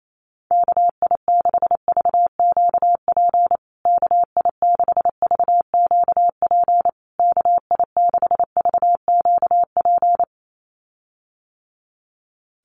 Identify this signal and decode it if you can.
Morse code, heard as a keyed tone, about 28 wpm, 700 Hz. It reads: KI6VQP KI6VQP KI6VQP
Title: USA Call Signs Repeated 3x 28wpm